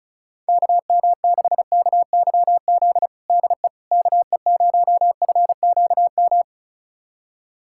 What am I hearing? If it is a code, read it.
Morse code, heard as a keyed tone, about 36 wpm, 700 Hz. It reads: KM6KYZ DE KE0FQM
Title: USA Callsign DE Callsign 35wpm